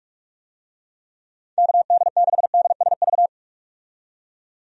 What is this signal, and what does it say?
Morse code, heard as a keyed tone, about 46 wpm, 700 Hz. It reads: KD6DSV
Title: USA Call Signs - 6 Characters - 45wpm